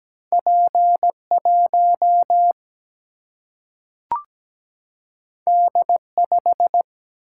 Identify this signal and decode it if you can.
Morse code, heard as a keyed tone, about 17 wpm, 700 Hz. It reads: P 1 D 5
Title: Single-Letter Single-Number 17wpm